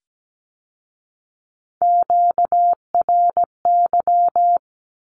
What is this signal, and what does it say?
Morse code, heard as a keyed tone, about 17 wpm, 700 Hz. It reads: QRY
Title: Q-Codes with Spoken Meaning - 17wpm